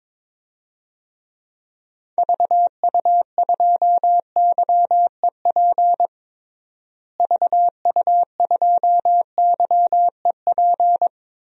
Morse code, heard as a keyed tone, about 22 wpm, 700 Hz. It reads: VU2YEP VU2YEP
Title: DXCC-100 Call Signs - 6 Characters - Repeated 2x - 22wpm